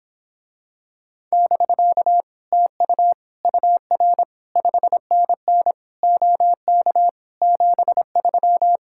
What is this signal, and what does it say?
Morse code, heard as a keyed tone, about 26 wpm, 700 Hz. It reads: <BK> TU UR 5NN OK 73